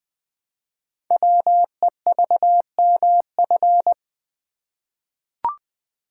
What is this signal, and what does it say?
Morse code, heard as a keyed tone, about 20 wpm, 700 Hz. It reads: WEVMF
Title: Code Groups - 5 Letters - 20wpm